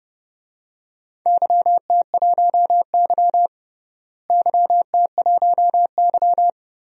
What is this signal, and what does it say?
Morse code, heard as a keyed tone, about 31 wpm, 700 Hz. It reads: YT1Y YT1Y
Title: DXCC-100 Call Signs - 4 Characters - Repeated 2x - 30wpm